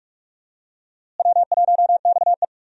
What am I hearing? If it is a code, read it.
Morse code, heard as a keyed tone, about 46 wpm, 700 Hz. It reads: W1XE
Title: USA Call Signs - 4 Characters - Repeated 3x - 45wpm